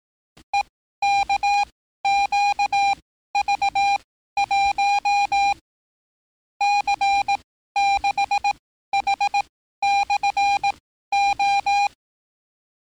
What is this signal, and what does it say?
Morse code, heard as a keyed tone, about 18 wpm, spaced as for 13 wpm, 800 Hz. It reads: EKQV1 C6H/O
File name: training / NAVPERS 11049to11060 / NAVPERS 11054 RA RB